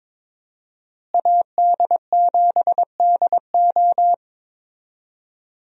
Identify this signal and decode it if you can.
Morse code, heard as a keyed tone, about 22 wpm, 700 Hz. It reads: AD7DO
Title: USA Call Signs - 5 Characters - Repeated 3x - 22wpm